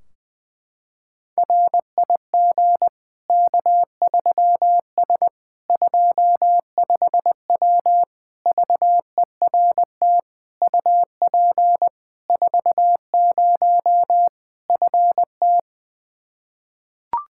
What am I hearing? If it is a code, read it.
Morse code, heard as a keyed tone, about 21 wpm, 700 Hz. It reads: RIG K3S 25W VERT UP 40 FT
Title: K1USN SST QSO Practice 20wpm